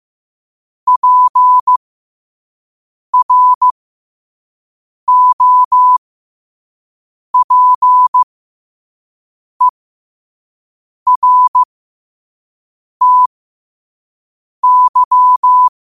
Morse code, heard as a keyed tone, about 15 wpm, spaced as for 5 wpm, 1000 Hz. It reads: PROPERTY
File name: qotdfiles / 5wpm